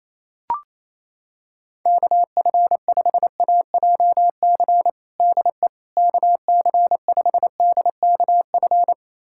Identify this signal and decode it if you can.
Morse code, heard as a keyed tone, about 27 wpm, 700 Hz. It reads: KF5AJC DE KC5DKF
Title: USA Callsign DE Callsign - 6 Characters - 28wpm